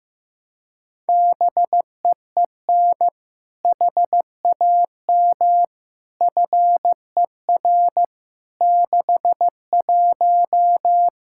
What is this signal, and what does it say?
Morse code, heard as a keyed tone, about 15 wpm, 700 Hz. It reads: BEEN HAM FER 61
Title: Practice QSO - Exchange 3 - More Info - 15wpm